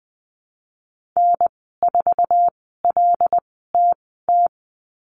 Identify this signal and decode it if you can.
Morse code, heard as a keyed tone, about 20 wpm, spaced as for 15 wpm, 700 Hz. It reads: N4LTT
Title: 088 Lesson 15g - Number 5 - Call Signs 20wpm